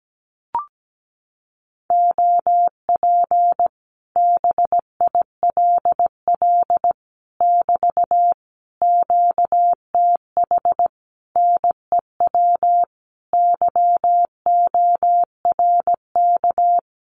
Morse code, heard as a keyed tone, about 17 wpm, 700 Hz. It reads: OP BILL <BT> QTH NEW YORK
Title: Common QSO Phrases Encore 17wpm